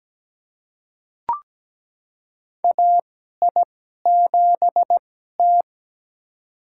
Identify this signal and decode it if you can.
Morse code, heard as a keyed tone, about 17 wpm, spaced as for 12 wpm, 700 Hz. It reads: AI7T
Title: USA Call Signs - 2x Character Spacing - 17wpm